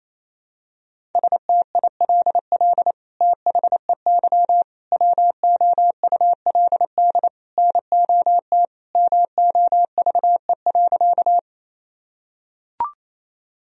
Morse code, heard as a keyed tone, about 28 wpm, 700 Hz. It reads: STILL THEY WOULD NOT MOVE.
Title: Sentences from Top 1000 Words - Review Part 1 - 28wpm